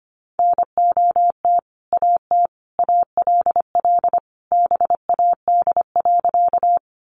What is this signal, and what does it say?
Morse code, heard as a keyed tone, about 25 wpm, 700 Hz. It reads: NOT AT ALL BAD.
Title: Sentences from Top 700 Words 25wpm